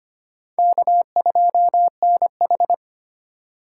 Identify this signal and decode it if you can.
Morse code, heard as a keyed tone, about 34 wpm, spaced as for 30 wpm, 700 Hz. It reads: K2NH
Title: Speed Racing - USA 1x2 Call Signs 25wpm